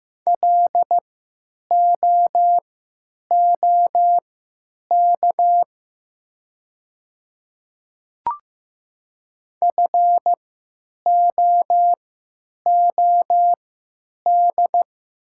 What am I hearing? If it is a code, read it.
Morse code, heard as a keyed tone, about 15 wpm, spaced as for 9 wpm, 690 Hz. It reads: LOOK FOOD
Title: 3x Character Spacing - Top 300 Words - 15wpm